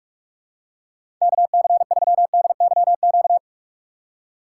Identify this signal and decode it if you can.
Morse code, heard as a keyed tone, about 46 wpm, 700 Hz. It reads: KC3DYQ
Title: USA Call Signs - 6x Character Spacing - 45wpm